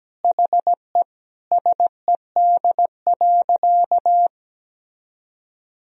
Morse code, heard as a keyed tone, about 17 wpm, 700 Hz. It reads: HE SED.
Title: Sentences from Top 300 Words and Abbreviations 17wpm